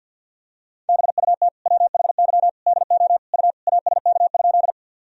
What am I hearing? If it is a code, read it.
Morse code, heard as a keyed tone, about 49 wpm, 700 Hz. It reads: BUT WHY DO U ASK?
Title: Sentences from Top 200 Words and Abbreviations 50wpm